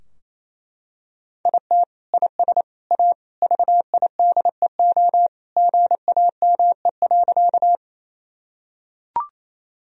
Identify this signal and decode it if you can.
Morse code, heard as a keyed tone, about 28 wpm, 700 Hz. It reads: IT IS A VIDEO GAME.
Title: Sentences from Top 1000 Words - Review Part 4 - 28wpm